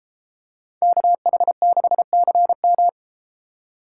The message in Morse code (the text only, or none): KH6CM